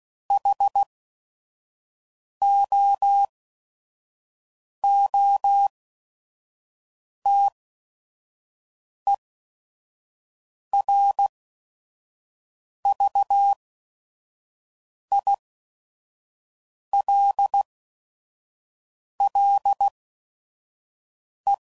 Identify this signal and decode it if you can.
Morse code, heard as a keyed tone, about 16 wpm, spaced as for 5 wpm, 800 Hz. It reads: HOOTERVILLE
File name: CW Code Groups @5WPM / fromSM's / Cities